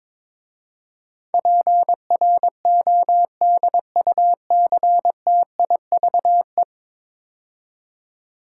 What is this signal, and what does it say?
Morse code, heard as a keyed tone, about 22 wpm, 700 Hz. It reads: PRODUCTIVE
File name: practice / 22wpm / 10 Letter Words - First 3 Letters Given - 22wpm